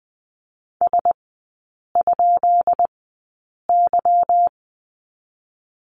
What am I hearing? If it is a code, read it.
Morse code, heard as a keyed tone, about 20 wpm, spaced as for 8 wpm, 700 Hz. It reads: S?Y
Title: 152 Lesson 25e - Letter p - Three Characters 20wpm